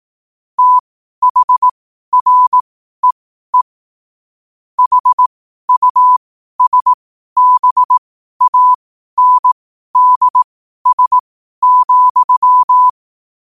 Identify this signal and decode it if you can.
Morse code, heard as a keyed tone, about 18 wpm, spaced as for 12 wpm, 1000 Hz. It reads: THREE HUSBANDS,
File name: qotdfiles / 12wpm